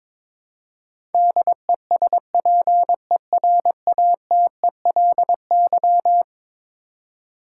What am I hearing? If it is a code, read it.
Morse code, heard as a keyed tone, about 22 wpm, 700 Hz. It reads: DESPERATELY
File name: practice / 22wpm / Eleven Letter Words - 22wpm